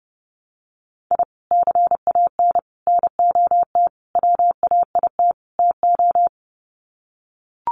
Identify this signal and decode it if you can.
Morse code, heard as a keyed tone, about 30 wpm, 700 Hz. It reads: I CAN NOT WAIT TO
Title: Sets of 5 Words Encore 30wpm